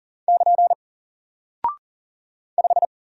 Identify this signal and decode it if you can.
Morse code, heard as a keyed tone, about 40 wpm, 700 Hz. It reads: <KN> 5
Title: Single Character (Letter:Number:Punctuation:Prosign) 40wpm